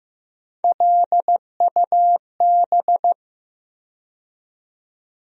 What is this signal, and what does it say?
Morse code, heard as a keyed tone, about 15 wpm, 700 Hz. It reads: LUB
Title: USA Call Sign Postfixes - 3 Letters 15wpm